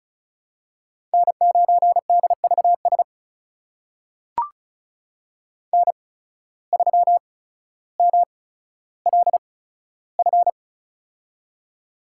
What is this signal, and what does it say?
Morse code, heard as a keyed tone, about 34 wpm, spaced as for 15 wpm, 700 Hz. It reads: N9DVS N3MLF
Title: USA Call Signs - 5 Characters - 8x Character Spacing - 35wpm